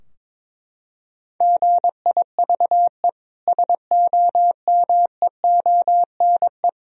Morse code, heard as a keyed tone, about 22 wpm, 700 Hz. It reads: GIVE SOMEONE
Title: Sentences from Top 1000 Words - Review Part 4 - 22wpm